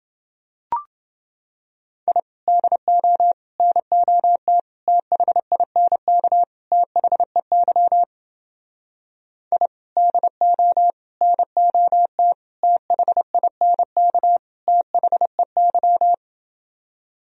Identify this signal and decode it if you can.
Morse code, heard as a keyed tone, about 29 wpm, 700 Hz. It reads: I DO NOT THINK THEY I DO NOT THINK THEY